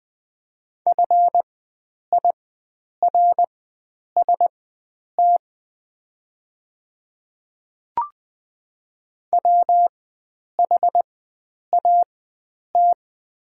Morse code, heard as a keyed tone, about 20 wpm, spaced as for 9 wpm, 690 Hz. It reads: FIRST WHAT
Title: 4x Character Spacing - Top 100 Words - 20wpm